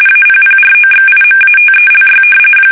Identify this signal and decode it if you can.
RTTY, about 45.45 baud. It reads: APPROXIMATELY
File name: fsk850 / usnavy3